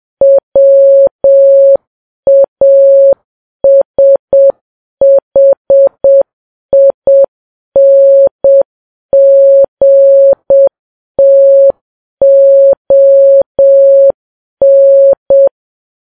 Morse code, heard as a keyed tone, about 7 wpm, 550 Hz. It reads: WASHINGTON
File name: Files Web tls / Morse on9fv / morse7kps